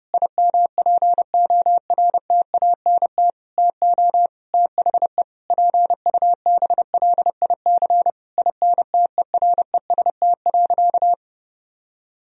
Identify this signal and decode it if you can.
Morse code, heard as a keyed tone, about 32 wpm, 700 Hz. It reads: IMPORTANT TO THE PUBLIC INTEREST.
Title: Sentences from Top 1000 Words - Review Part 4 - 30wpm